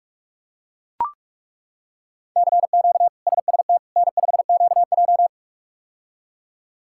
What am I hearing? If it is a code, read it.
Morse code, heard as a keyed tone, about 42 wpm, 700 Hz. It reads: CQ SST N5QJ
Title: K1USN SST QSO Practice - 45wpm